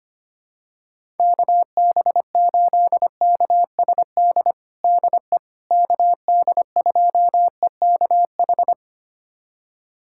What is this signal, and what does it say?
Morse code, heard as a keyed tone, about 25 wpm, 700 Hz. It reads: KB8KSD DE KD2EKH
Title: USA Callsign DE Callsign 25wpm